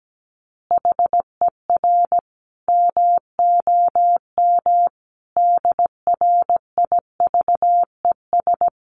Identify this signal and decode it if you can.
Morse code, heard as a keyed tone, about 17 wpm, 700 Hz. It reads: HER MOM DRIVES
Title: Easy Phrases 17wpm